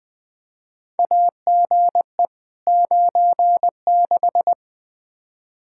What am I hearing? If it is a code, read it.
Morse code, heard as a keyed tone, about 20 wpm, 700 Hz. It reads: AGE 96
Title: Practice QSO - Exchange 3 - More Info - 20wpm